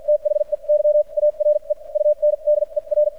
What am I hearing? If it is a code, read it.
Morse code, heard as a keyed tone, about 50 wpm, spaced as for 40 wpm, 600 Hz. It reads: THEY ARE UNDER
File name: archived / CRSnet